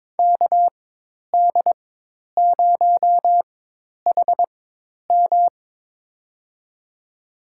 Morse code, heard as a keyed tone, about 22 wpm, spaced as for 10 wpm, 700 Hz. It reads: KD0HM
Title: USA Call Signs - 5 Characters - 4x Character Spacing - 22wpm